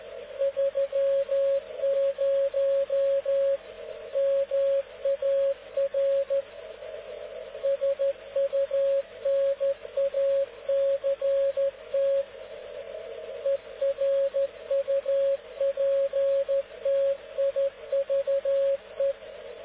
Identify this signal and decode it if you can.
Morse code, heard as a keyed tone, about 13 wpm, 550 Hz, with radio noise. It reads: 30 MAR SUNACT ERUPTIVE